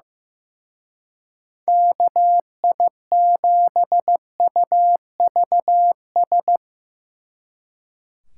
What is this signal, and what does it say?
Morse code, heard as a keyed tone, about 17 wpm, 700 Hz. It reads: KI7UVS
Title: Speed Racing - USA 2x3 Call Signs 15wpm